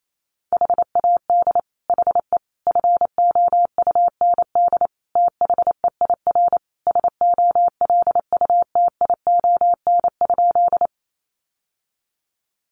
Morse code, heard as a keyed tone, about 28 wpm, 700 Hz. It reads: HAD HE FOUND THEIR SOLUTION?
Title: Sentences from Top 1300 Words 28wpm